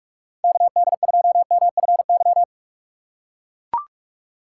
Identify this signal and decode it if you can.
Morse code, heard as a keyed tone, about 42 wpm, 700 Hz. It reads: KD2MFY
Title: USA Call Signs - 6 Characters - 2x Character Spacing - 45wpm